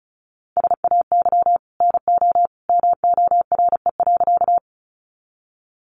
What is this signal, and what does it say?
Morse code, heard as a keyed tone, about 34 wpm, 700 Hz. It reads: SAY NO MORE.
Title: Sentences from Top 200 Words 35wpm